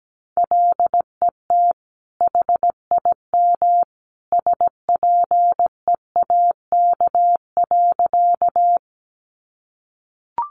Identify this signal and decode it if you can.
Morse code, heard as a keyed tone, about 17 wpm, 700 Hz. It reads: LET HIM SPEAK.
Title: Sentences from Top 1100 Words 17wpm